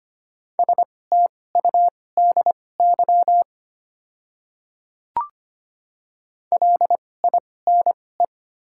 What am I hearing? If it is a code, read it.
Morse code, heard as a keyed tone, about 25 wpm, spaced as for 18 wpm, 700 Hz. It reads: STUDY LINE
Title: 2x Character Spacing - Top 300 Words - 25wpm